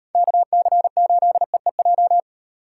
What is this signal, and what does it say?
Morse code, heard as a keyed tone, about 38 wpm, 700 Hz. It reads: KC8EEJ